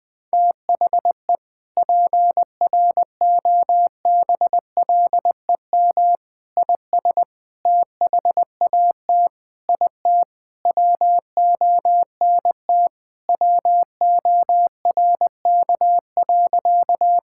Morse code, heard as a keyed tone, about 20 wpm, 700 Hz. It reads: THE PROBLEM IS THAT IT WONT WORK.
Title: Sentences with 3-Word Phrases - 20wpm